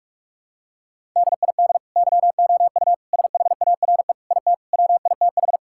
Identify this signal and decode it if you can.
Morse code, heard as a keyed tone, about 41 wpm, 700 Hz. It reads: DID YOU SHARE IT WITH
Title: Sentences from Top 1000 Words - Review Part 6 - 45wpm